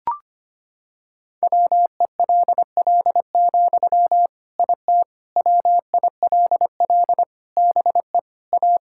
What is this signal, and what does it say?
Morse code, heard as a keyed tone, about 25 wpm, 700 Hz. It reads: WELL, IT WILL BE A
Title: Sentences from Top 1000 Words - Review Part 6 - 25wpm